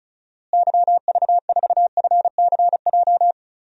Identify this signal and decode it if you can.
Morse code, heard as a keyed tone, about 36 wpm, 700 Hz. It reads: YV4FCJ
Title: DXCC-100 Call Signs - 6 Characters - Repeated 3x - 35wpm